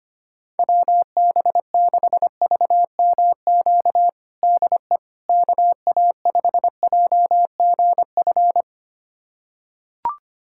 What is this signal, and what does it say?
Morse code, heard as a keyed tone, about 25 wpm, 700 Hz. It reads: WB6VMQ DE KA5JGF
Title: USA Callsign DE Callsign - 6 Characters - 25wpm